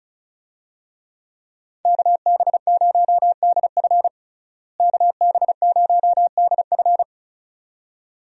Morse code, heard as a keyed tone, about 36 wpm, 700 Hz. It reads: KB0DF KB0DF
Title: USA Call Signs - 5 Characters - Repeated 2x - 35wpm